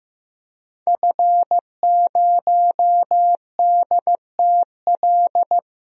Morse code, heard as a keyed tone, about 16 wpm, 700 Hz. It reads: F0DTL
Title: DXCC-100 Call Signs - 5 Characters - Repeated 3x - 15wpm